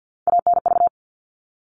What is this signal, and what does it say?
Morse code, heard as a keyed tone, about 50 wpm, 700 Hz. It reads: ANV
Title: All Cut Numbers - 3-Digit 50wpm